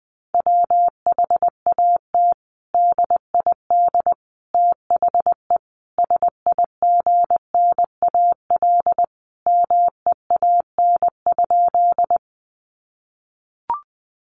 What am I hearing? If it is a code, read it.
Morse code, heard as a keyed tone, about 20 wpm, 700 Hz. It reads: WHAT DID THE SIGNAL MEAN?
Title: Sentences from Top 1400 Words 20wpm